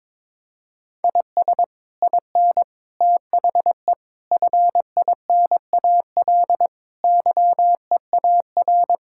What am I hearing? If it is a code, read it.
Morse code, heard as a keyed tone, about 22 wpm, 700 Hz. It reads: IS IN THE FINAL YEAR
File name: practice / 22wpm / Sets of 8 Words - 22wpm